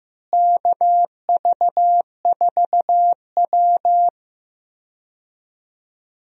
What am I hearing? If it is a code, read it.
Morse code, heard as a keyed tone, about 15 wpm, 700 Hz. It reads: KV4W
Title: USA Call Signs - 4 Characters - Repeated 3x - 15wpm